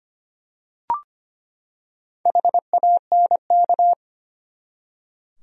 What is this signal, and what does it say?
Morse code, heard as a keyed tone, about 25 wpm, 700 Hz. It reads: HANK
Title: USA Common Names - Max 4 Letters - 25wpm